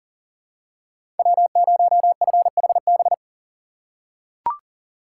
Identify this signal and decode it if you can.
Morse code, heard as a keyed tone, about 40 wpm, 700 Hz. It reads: W0FHB
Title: USA Call Signs - 5 Characters - 4x Character Spacing - 40wpm